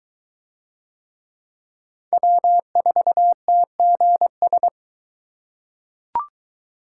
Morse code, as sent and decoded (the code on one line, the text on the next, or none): .-- ....- - --. ...
W4TGS